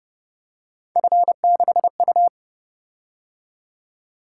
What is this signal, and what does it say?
Morse code, heard as a keyed tone, about 29 wpm, 700 Hz. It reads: F6U